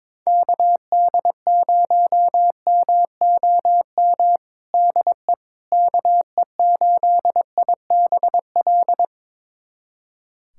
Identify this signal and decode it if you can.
Morse code, heard as a keyed tone, about 22 wpm, 700 Hz. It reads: KD0MOM DE KE8IBL
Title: USA Callsign DE Callsign - 6 Characters - 22wpm